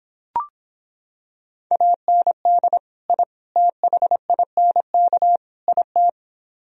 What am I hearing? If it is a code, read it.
Morse code, heard as a keyed tone, about 26 wpm, 700 Hz. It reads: AND I THINK IT